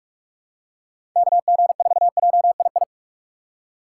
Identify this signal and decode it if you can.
Morse code, heard as a keyed tone, about 41 wpm, 700 Hz. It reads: KG4JII
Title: USA Call Signs - 6 Characters - 4x Character Spacing - 45wpm